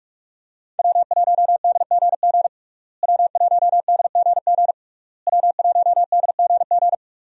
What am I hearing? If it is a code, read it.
Morse code, heard as a keyed tone, about 43 wpm, 700 Hz. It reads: W1DGG W1DGG W1DGG
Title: USA Call Signs - 5 Characters - Repeated 3x - 45wpm